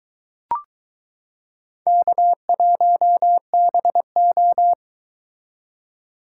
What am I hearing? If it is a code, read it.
Morse code, heard as a keyed tone, about 22 wpm, 700 Hz. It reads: K1BO